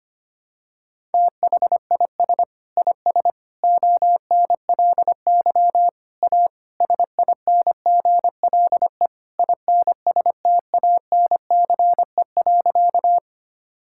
Morse code, heard as a keyed tone, about 25 wpm, 700 Hz. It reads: THIS IS ONLY A SINGLE INSTANCE.
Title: Sentences from Top 1000 Words - Review Part 5 - 25wpm